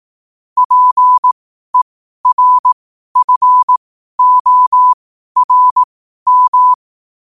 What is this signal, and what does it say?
Morse code, heard as a keyed tone, about 18 wpm, spaced as for 12 wpm, 1000 Hz. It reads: PERFORM
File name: qotdfiles / 12wpm